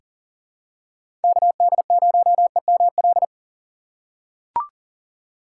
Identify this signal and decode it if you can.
Morse code, heard as a keyed tone, about 38 wpm, 700 Hz. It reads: KD0EML
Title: USA Call Signs - 6 Characters - 3x Character Spacing - 40wpm